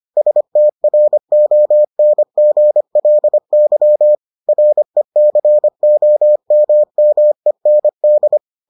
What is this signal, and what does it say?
Morse code, heard as a keyed tone, about 25 wpm, 600 Hz. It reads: STRONGLY RECOMMEND